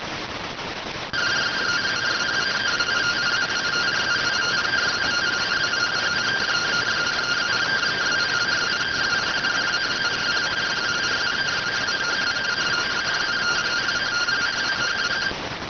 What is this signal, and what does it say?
Two-tone teleprinter signal, about 170 Hz shift.